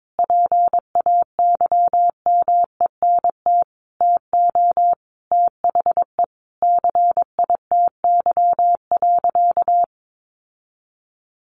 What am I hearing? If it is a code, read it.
Morse code, heard as a keyed tone, about 22 wpm, 700 Hz. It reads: PAYMENT TO THE CITY.
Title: Sentences from Top 1400 Words 22wpm